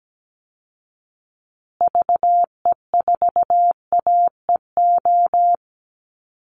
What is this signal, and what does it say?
Morse code, heard as a keyed tone, about 17 wpm, 700 Hz. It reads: VE4AEO
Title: Call Signs Repeated 3x - 17wpm